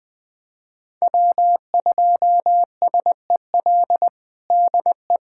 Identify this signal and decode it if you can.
Morse code, heard as a keyed tone, about 21 wpm, 700 Hz. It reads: W2SEL DE
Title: USA Callsign DE Callsign - 5 Characters - 20wpm